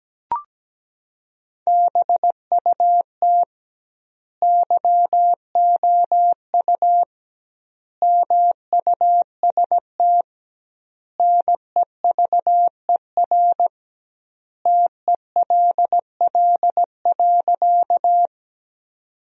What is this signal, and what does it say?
Morse code, heard as a keyed tone, about 17 wpm, 700 Hz. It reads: BUT YOU MUST NEVER TELL.
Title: 2x Word-Spacing - Sentences from Top 300 Words - 17wpm